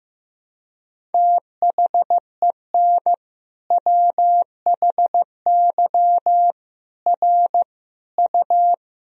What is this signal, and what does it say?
Morse code, heard as a keyed tone, about 15 wpm, 700 Hz. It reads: THEN WHY R U
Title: Sentences from Top 200 Words and Abbreviations 15wpm